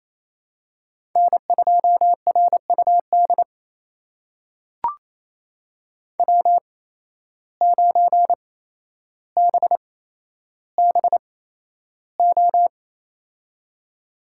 Morse code, heard as a keyed tone, about 28 wpm, spaced as for 12 wpm, 700 Hz. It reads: N2RUD W9BBO
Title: USA Call Signs - 5 Characters - 8x Character Spacing - 28wpm